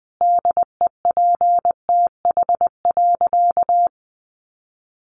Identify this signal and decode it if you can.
Morse code, heard as a keyed tone, about 20 wpm, 700 Hz. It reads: DEPTH.
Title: Sentences from Top 1200 Words 20wpm